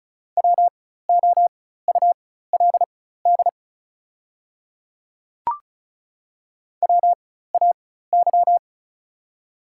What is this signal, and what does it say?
Morse code, heard as a keyed tone, about 34 wpm, spaced as for 16 wpm, 700 Hz. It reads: WOULD WAY
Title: 4x Character Spacing - Top 100 Words - 35wpm